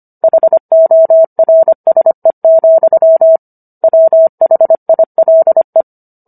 Morse code, heard as a keyed tone, about 25 wpm, 650 Hz. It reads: HORSE, WHILE